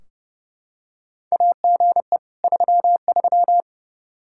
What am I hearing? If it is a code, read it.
Morse code, heard as a keyed tone, about 30 wpm, 700 Hz. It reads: AGE 33
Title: Practice QSO - Exchange 3 - More Info - 30wpm